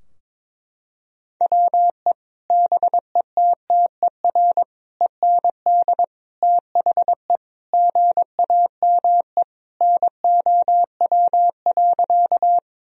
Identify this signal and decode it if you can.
Morse code, heard as a keyed tone, about 22 wpm, 700 Hz. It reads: WE BETTER END THE GAME NOW.
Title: Sentences from Top 1000 Words - Review Part 3 - 22wpm